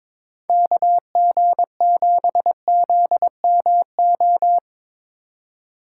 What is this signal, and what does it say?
Morse code, heard as a keyed tone, about 22 wpm, 700 Hz. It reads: KG7ZMO
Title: USA Call Signs - 6 Characters - 2x Character Spacing - 22wpm